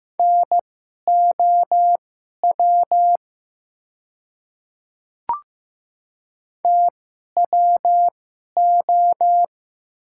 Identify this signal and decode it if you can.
Morse code, heard as a keyed tone, about 15 wpm, spaced as for 11 wpm, 700 Hz. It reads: NOW TWO
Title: 2x Character Spacing - Top 100 Words - 15wpm